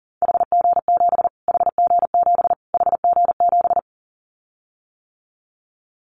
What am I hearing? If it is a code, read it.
Morse code, heard as a keyed tone, about 40 wpm, 700 Hz. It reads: HG7 HG7 HG7
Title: Callsign Prefixes for Top 25 DXCC - Repeated 3x - 40wpm